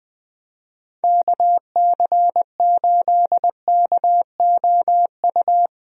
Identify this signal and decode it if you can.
Morse code, heard as a keyed tone, about 20 wpm, 700 Hz. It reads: KC8KOU
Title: USA Call Signs - 6 Characters - Repeated 3x - 20wpm